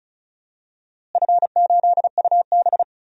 Morse code, heard as a keyed tone, about 36 wpm, 700 Hz. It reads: F8UB
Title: DXCC-100 Call Signs - 4 Characters - Repeated 2x - 35wpm